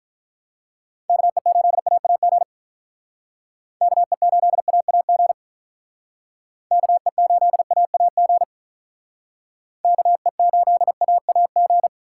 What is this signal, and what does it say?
Morse code, heard as a keyed tone, about 47 wpm, 700 Hz. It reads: KE8AAG KE8AAG KE8AAG KE8AAG